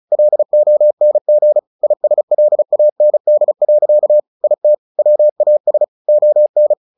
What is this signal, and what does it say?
Morse code, heard as a keyed tone, about 34 wpm, 600 Hz. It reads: LONG ISLAND. IT WAS ON
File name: the old man and the sea 2 35wpm